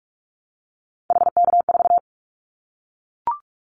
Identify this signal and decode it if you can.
Morse code, heard as a keyed tone, about 44 wpm, 700 Hz. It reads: HK4
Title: Callsign Prefixes for Top 100 DXCC - 45wpm